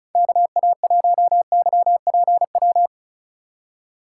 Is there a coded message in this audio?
Yes: Morse code from 0.2 to 2.9 s, heard as a keyed tone, about 45 wpm, 700 Hz. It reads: KA1YPW